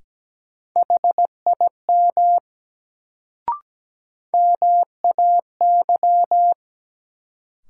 Morse code, heard as a keyed tone, about 17 wpm, 700 Hz. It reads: HIM MAY
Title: Sentences from Top 100 Words - New Word Drill - 17wpm